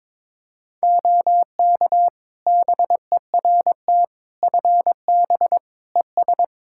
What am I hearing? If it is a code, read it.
Morse code, heard as a keyed tone, about 22 wpm, 700 Hz. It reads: OK BERT FB ES
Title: Practice QSO - Exchange 2 - Info - 22wpm